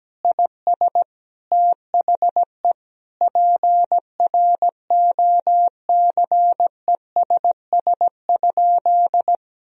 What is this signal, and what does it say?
Morse code, heard as a keyed tone, about 17 wpm, 700 Hz. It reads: IS THE PROCESS?
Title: Sentences from Top 1000 Words - Review Part 3 - 17wpm